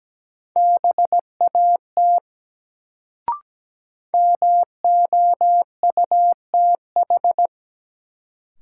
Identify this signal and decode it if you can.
Morse code, heard as a keyed tone, about 17 wpm, 700 Hz. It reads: BAT MOUTH
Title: Sentences from Top 1500 Words - New Word Drill Alt - 17wpm